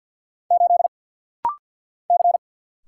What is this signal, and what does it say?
Morse code, heard as a keyed tone, about 49 wpm, 700 Hz. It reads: 8 C
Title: Single Letter-Number - Rapid-Fire 50wpm